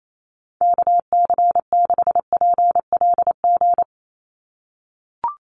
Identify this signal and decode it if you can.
Morse code, heard as a keyed tone, about 28 wpm, 700 Hz. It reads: KC6PLG
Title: USA Call Signs 28wpm